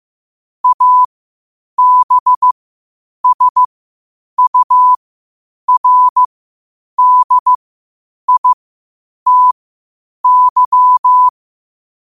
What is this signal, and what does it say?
Morse code, heard as a keyed tone, about 15 wpm, spaced as for 8 wpm, 1000 Hz. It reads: ABSURDITY
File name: qotdfiles / 8wpm